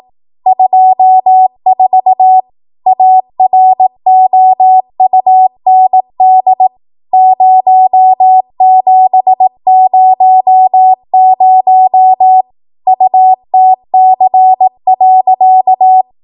Morse code, heard as a keyed tone, about 18 wpm, 750 Hz. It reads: 24 AROUND 0700 UTC.
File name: Morse / 18WPM